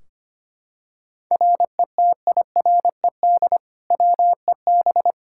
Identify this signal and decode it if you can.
Morse code, heard as a keyed tone, about 25 wpm, 700 Hz. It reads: RETIRED WEB
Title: Practice QSO - Exchange 3 - More Info - 25wpm